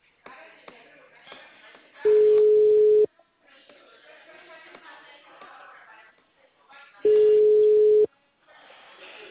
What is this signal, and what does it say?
Ringback tone (Europe, 425 Hz).